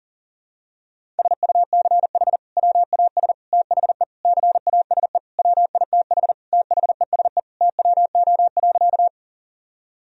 Morse code, heard as a keyed tone, about 40 wpm, 700 Hz. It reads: SUCH WAS THE CASE WITH THESE TWO.
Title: Sentences from Top 1000 Words - Review Part 1 - 40wpm